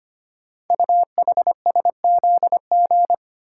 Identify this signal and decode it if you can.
Morse code, heard as a keyed tone, about 25 wpm, 700 Hz. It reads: UHSZG
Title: Code Groups - 5 Letters - 25wpm